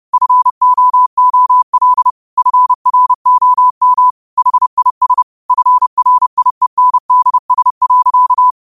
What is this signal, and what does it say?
Morse code, heard as a keyed tone, about 30 wpm, 1000 Hz. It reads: FOOL FROM HIS FRIENDS.
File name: qotdfiles / 30wpm